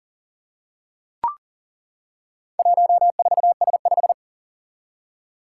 Morse code, heard as a keyed tone, about 40 wpm, 700 Hz. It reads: 14S5
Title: Four Challenging Letter-Number Combinations 40wpm